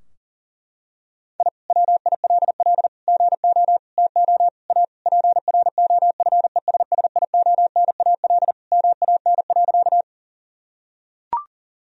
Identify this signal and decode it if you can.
Morse code, heard as a keyed tone, about 40 wpm, 700 Hz. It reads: I WILL GO TO A PROFESSIONAL MAN.
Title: Sentences from Top 1000 Words - Review Part 4 - 40wpm